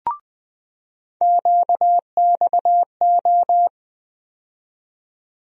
Morse code, heard as a keyed tone, about 20 wpm, 700 Hz. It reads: QXO
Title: USA Call Sign Postfixes - 3 Letters 20wpm